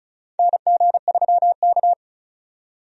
Morse code, heard as a keyed tone, about 36 wpm, 690 Hz. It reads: NG3K
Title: USA Call Signs - 4 Characters - 35wpm